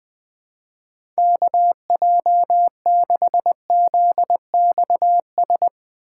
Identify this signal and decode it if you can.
Morse code, heard as a keyed tone, about 20 wpm, 700 Hz. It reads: KJ6ZXS
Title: USA Call Signs - 6x Character Spacing - 20wpm